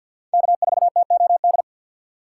Morse code, heard as a keyed tone, about 51 wpm, 700 Hz. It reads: K4TOD
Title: USA Call Signs - 5 Characters - 3x Character Spacing - 50wpm